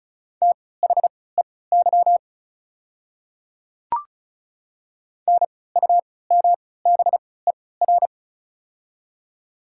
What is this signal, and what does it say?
Morse code, heard as a keyed tone, about 36 wpm, spaced as for 20 wpm, 700 Hz. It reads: THEY NUMBER
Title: 3x Character Spacing - Top 300 Words - 35wpm